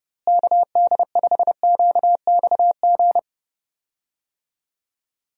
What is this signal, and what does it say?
Morse code, heard as a keyed tone, about 30 wpm, 700 Hz. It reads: KD5QXG
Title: USA Call Signs - 6 Characters - Repeated 3x - 30wpm